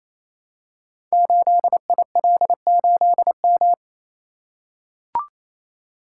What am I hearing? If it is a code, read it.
Morse code, heard as a keyed tone, about 27 wpm, 700 Hz. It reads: 8IL8M
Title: Code Groups - 5 Letter:Numbers - 28wpm